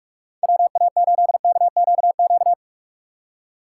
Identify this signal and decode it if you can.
Morse code, heard as a keyed tone, about 60 wpm, spaced as for 52 wpm, 700 Hz. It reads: WA8KQQ
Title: Speed Racing - USA Call Signs 45wpm